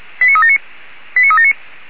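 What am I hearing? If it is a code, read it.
Five-tone selective call (ZVEI), heard as quick runs of five tones, tones of about 70 ms. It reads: 89379 89379
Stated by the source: ZVEI five-tone selective call received on an SDR.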